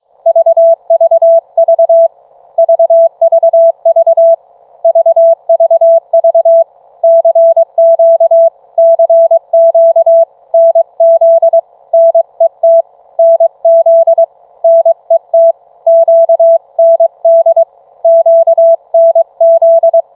Morse code, heard as a keyed tone, about 23 wpm, 660 Hz. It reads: VVV VVV VVV CQ CQ NZ NET NZ NET QND QNZ